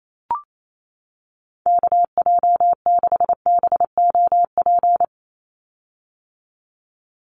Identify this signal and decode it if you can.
Morse code, heard as a keyed tone, about 28 wpm, 700 Hz. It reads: KJ6BOP
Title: USA Call Signs 28wpm